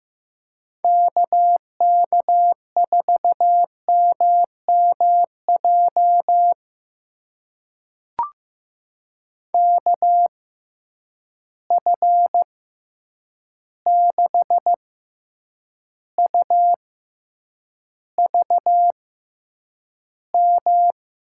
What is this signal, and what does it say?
Morse code, heard as a keyed tone, about 15 wpm, spaced as for 8 wpm, 700 Hz. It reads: KK4MMJ KF6UVM
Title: USA Call Signs - 6 Characters - 6x Character Spacing - 15wpm